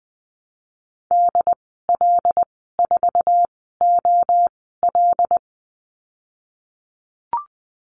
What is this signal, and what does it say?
Morse code, heard as a keyed tone, about 19 wpm, spaced as for 14 wpm, 700 Hz. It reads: DL4OL
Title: 082 Lesson 14g - Number 2 - Call Signs 20wpm